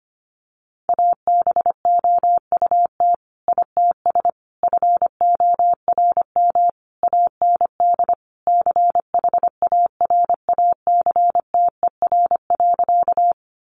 Morse code, heard as a keyed tone, about 25 wpm, 700 Hz. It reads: ABOUT ITS FORM AND CHARACTER.
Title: Sentences from Top 1200 Words 25wpm